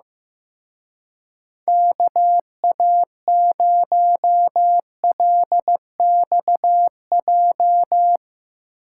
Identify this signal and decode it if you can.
Morse code, heard as a keyed tone, about 16 wpm, 700 Hz. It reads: KA0LXJ
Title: Speed Racing - USA 2x3 Call Signs 15wpm